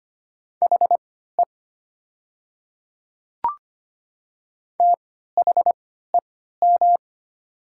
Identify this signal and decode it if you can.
Morse code, heard as a keyed tone, about 25 wpm, spaced as for 14 wpm, 700 Hz. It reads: HE THEM
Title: 3x Character Spacing - Top 100 Words - 25wpm